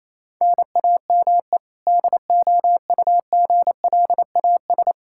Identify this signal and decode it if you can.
Morse code, heard as a keyed tone, about 28 wpm, 700 Hz. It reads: NAME DOUGLAS
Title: Practice QSO - Exchange 1 - Beginning - 28wpm